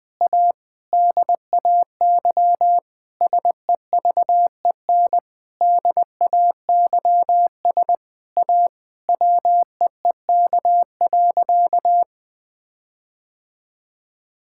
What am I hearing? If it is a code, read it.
Morse code, heard as a keyed tone, about 21 wpm, 700 Hz. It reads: A DAY SEVEN DAYS A WEEK.
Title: Sentences with 7-Word Phrases - 20wpm